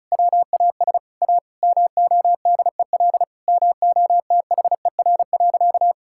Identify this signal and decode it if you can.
Morse code, heard as a keyed tone, about 36 wpm, 700 Hz. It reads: WAS A MODEL MOTHER.
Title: Sentences from Top 1000 Words - Review Part 5 - 35wpm